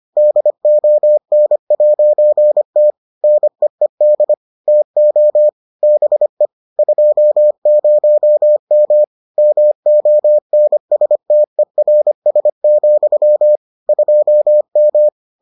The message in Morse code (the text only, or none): DON'T NEED TO BE 20M MONSTERS, 2M